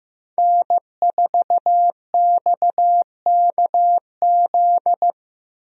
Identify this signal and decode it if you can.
Morse code, heard as a keyed tone, about 15 wpm, 700 Hz. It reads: N4XKZ
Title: USA Call Signs - 5 Characters - Repeated 2x - 15wpm